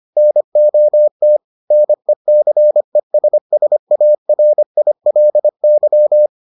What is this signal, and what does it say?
Morse code, heard as a keyed tone, about 25 wpm, 600 Hz. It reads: NOT NECESSARILY